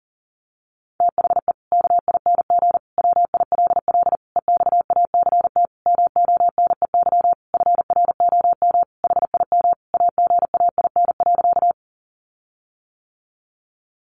Morse code, heard as a keyed tone, about 40 wpm, 700 Hz. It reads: THE KING WILL EXACT MONEY FROM HIM AGAIN.
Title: Sentences from Top 1300 Words 40wpm